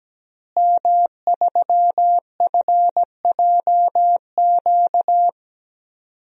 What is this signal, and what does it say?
Morse code, heard as a keyed tone, about 17 wpm, 700 Hz. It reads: M3FJQ
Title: DXCC-100 Call Signs - 5 Characters - Repeated 2x - 17wpm